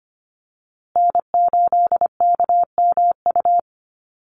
Morse code, heard as a keyed tone, about 25 wpm, 700 Hz. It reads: N8KMU
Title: USA Call Signs Repeated 2x 25wpm